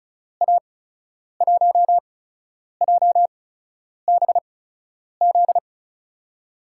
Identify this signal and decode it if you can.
Morse code, heard as a keyed tone, about 36 wpm, spaced as for 10 wpm, 700 Hz. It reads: A1JBZ
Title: USA Call Signs - 6 Characters - 8x Character Spacing - 35wpm